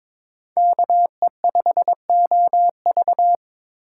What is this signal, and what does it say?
Morse code, heard as a keyed tone, about 26 wpm, 700 Hz. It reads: KE5OV
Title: Speed Racing - USA 2x2 Call Signs 22wpm